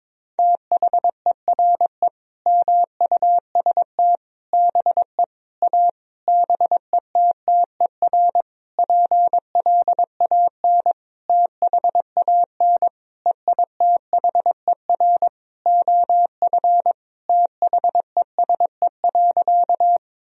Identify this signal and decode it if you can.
Morse code, heard as a keyed tone, about 22 wpm, 700 Hz. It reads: THERE MUST BE A BETTER PLAN THAN EITHER OF THESE.
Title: Sentences from Top 1000 Words - Review Part 4 - 22wpm